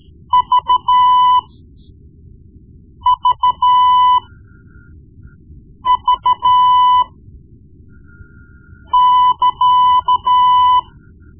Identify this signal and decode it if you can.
Morse code, heard as a keyed tone, about 12 wpm, spaced as for 4 wpm, 1000 Hz. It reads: VVV<KA>
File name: Morse Lessons / Morse Lessons / Tape-1B